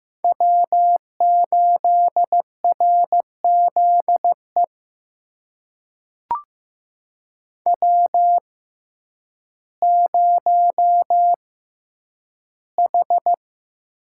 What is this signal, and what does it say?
Morse code, heard as a keyed tone, about 15 wpm, spaced as for 9 wpm, 700 Hz. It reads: W8RZE W0H
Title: USA Call Signs - 5 Characters - 6x Character Spacing - 15wpm